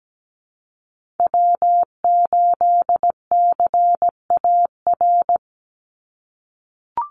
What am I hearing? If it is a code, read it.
Morse code, heard as a keyed tone, about 17 wpm, 700 Hz. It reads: W8CAR
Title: Call Signs - 17wpm